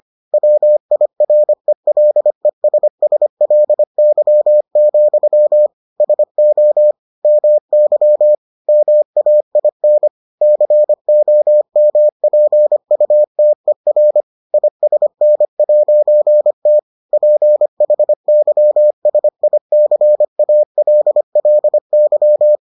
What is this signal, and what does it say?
Morse code, heard as a keyed tone, about 25 wpm, 600 Hz. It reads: WIRELESSLY, SO MY MAIN COMPUTER ISN'T PHYSICALLY